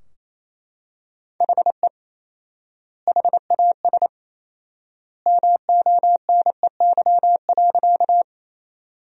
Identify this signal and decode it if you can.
Morse code, heard as a keyed tone, about 28 wpm, 700 Hz. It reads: HE HAS MONEY.
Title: 4x Word-Spacing - Sentences from Top 400 Words - 28wpm